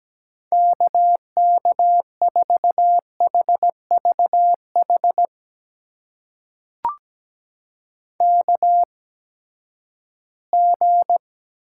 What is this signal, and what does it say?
Morse code, heard as a keyed tone, about 17 wpm, spaced as for 12 wpm, 700 Hz. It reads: KK4HVH KG
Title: USA Call Signs - 6 Characters - 8x Character Spacing - 17wpm